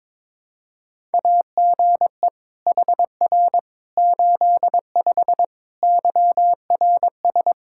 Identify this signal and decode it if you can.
Morse code, heard as a keyed tone, about 21 wpm, 700 Hz. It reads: AGE HR 85 YRS
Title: Practice QSO - Exchange 3 - More Info - 22wpm